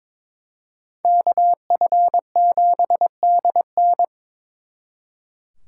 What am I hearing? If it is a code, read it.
Morse code, heard as a keyed tone, about 26 wpm, 700 Hz. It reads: KF7DN
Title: Speed Racing - USA 2x2 Call Signs 22wpm